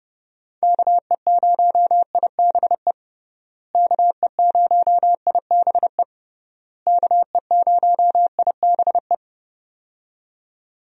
Morse code, heard as a keyed tone, about 30 wpm, 700 Hz. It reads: KE0IBE KE0IBE KE0IBE
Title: USA Call Signs - 6 Characters - Repeated 3x - 30wpm